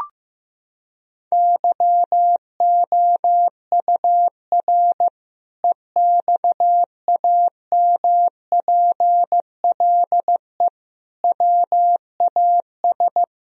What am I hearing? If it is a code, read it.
Morse code, heard as a keyed tone, about 16 wpm, 700 Hz. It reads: YOUR EXAMPLE WAS
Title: Sentences from Top 1000 Words - Review Part 1 - 15wpm